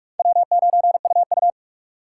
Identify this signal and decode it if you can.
Morse code, heard as a keyed tone, about 46 wpm, 700 Hz. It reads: W9UU
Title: USA Call Signs - 4 Characters - 2x Character Spacing - 45wpm